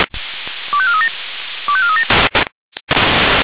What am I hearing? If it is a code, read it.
Five-tone selective call (ZVEI), heard as quick runs of five tones, tones of about 70 ms. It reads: 25437 25437